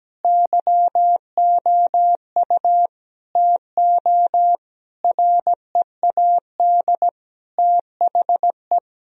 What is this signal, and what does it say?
Morse code, heard as a keyed tone, about 17 wpm, 700 Hz. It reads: YOU TO READ THE
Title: Sentences from Top 1000 Words - Review Part 1 - 17wpm